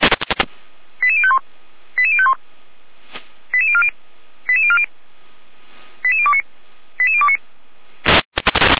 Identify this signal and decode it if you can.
Five-tone selective call (ZVEI), heard as quick runs of five tones, tones of about 70 ms. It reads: 88051 88051 88040 88040 88029 88029